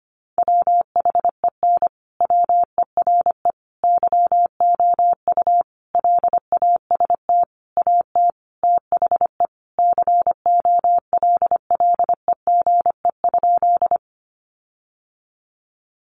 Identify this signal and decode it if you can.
Morse code, heard as a keyed tone, about 25 wpm, 700 Hz. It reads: WHEN WERE YOU LAST AT THE COLLEGE?
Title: Sentences from Top 1300 Words 25wpm